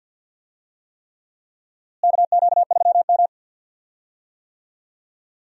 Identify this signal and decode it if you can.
Morse code, heard as a keyed tone, about 52 wpm, 700 Hz. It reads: KQ3M
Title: USA Call Signs - 4 Characters - Repeated 3x - 50wpm